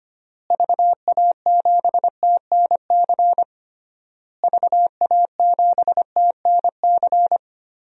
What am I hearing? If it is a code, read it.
Morse code, heard as a keyed tone, about 25 wpm, 700 Hz. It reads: VA7TNC VA7TNC
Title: DXCC-100 Call Signs - 6 Characters - Repeated 3x - 25wpm